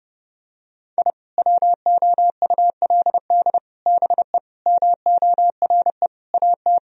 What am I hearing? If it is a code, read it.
Morse code, heard as a keyed tone, about 30 wpm, 700 Hz. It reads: I WOULD BE MORE AT
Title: Sentences from Top 1000 Words - Review Part 1 - 30wpm